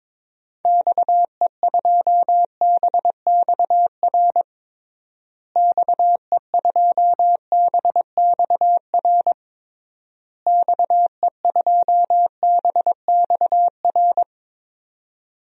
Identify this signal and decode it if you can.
Morse code, heard as a keyed tone, about 22 wpm, 700 Hz. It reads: XE2BXR XE2BXR XE2BXR
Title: DXCC-100 Call Signs - 6 Characters - Repeated 3x - 22wpm